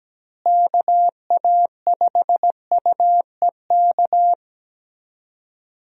Morse code, heard as a keyed tone, about 17 wpm, 700 Hz. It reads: KA5UEK
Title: USA Call Signs - 6 Characters - Repeated 3x - 17wpm